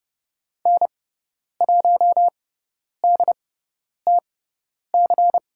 Morse code, heard as a keyed tone, about 29 wpm, spaced as for 10 wpm, 700 Hz. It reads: N1DTC
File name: Lesson 17g - Letter u - Call Signs 30wpm